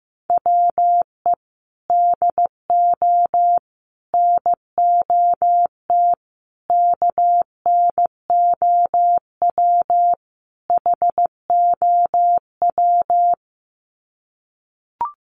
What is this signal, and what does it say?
Morse code, heard as a keyed tone, about 15 wpm, 700 Hz. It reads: WE DO NOT KNOW HOW
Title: Sets of 5 Words Encore 15wpm